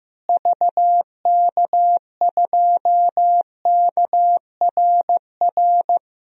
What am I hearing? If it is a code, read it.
Morse code, heard as a keyed tone, about 15 wpm, 700 Hz. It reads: VK2KRR
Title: DXCC-100 Call Signs - 6 Characters - Repeated 2x - 15wpm